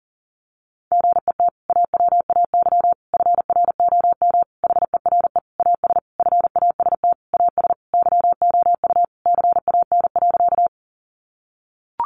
Morse code, heard as a keyed tone, about 40 wpm, 700 Hz. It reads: GET AWAY FROM HERE AS FAST AS YOU CAN.
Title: Sentences from Top 600 Words 40wpm